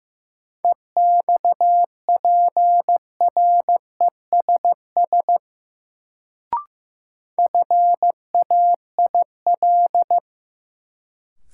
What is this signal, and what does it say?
Morse code, heard as a keyed tone, about 15 wpm, 700 Hz. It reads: EXPRESS FAIL
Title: Sentences from Top 1100 Words - New Word Drill Alt - 15wpm